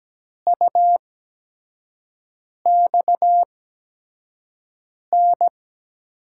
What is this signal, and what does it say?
Morse code, heard as a keyed tone, about 17 wpm, spaced as for 5 wpm, 700 Hz. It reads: UXN
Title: USA Call Signs - 8x Character Spacing - 17wpm